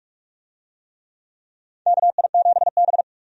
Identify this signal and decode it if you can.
Morse code, heard as a keyed tone, about 46 wpm, 700 Hz. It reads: KI7B
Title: USA Call Signs - 4 Characters - 45wpm